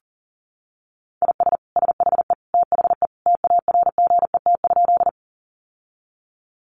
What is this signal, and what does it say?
Morse code, heard as a keyed tone, about 40 wpm, 700 Hz. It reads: IS SHE THE TARGET?
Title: Sentences from Top 1200 Words 40wpm